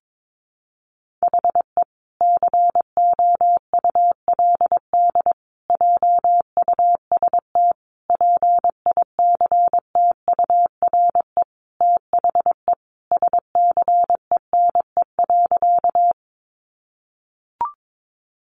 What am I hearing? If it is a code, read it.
Morse code, heard as a keyed tone, about 22 wpm, 700 Hz. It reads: HE COULD JUST PICTURE THE SCENE.
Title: Sentences from Top 1100 Words 22wpm